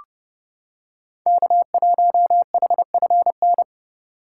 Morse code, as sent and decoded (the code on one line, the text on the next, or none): -.- .---- .... ..-. -.
K1HFN